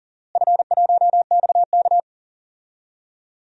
Morse code, heard as a keyed tone, about 40 wpm, 700 Hz. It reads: F1XK
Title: DXCC-100 Call Signs - 4 Characters - Repeated 2x - 40wpm